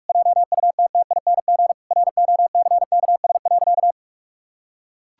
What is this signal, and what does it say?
Morse code, heard as a keyed tone, about 41 wpm, 700 Hz. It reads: JUTTING ROCKS.
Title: Pangram Challenge v2 - 45wpm